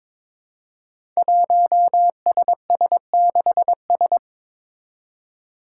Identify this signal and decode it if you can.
Morse code, heard as a keyed tone, about 22 wpm, 700 Hz. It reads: 1SS6S
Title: Five Challenging Letter-Number Combinations 22wpm